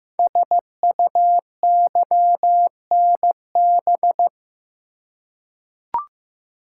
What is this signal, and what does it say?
Morse code, heard as a keyed tone, about 15 wpm, 700 Hz. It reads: SUYNB
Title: Code Groups - 5 Letters - 15wpm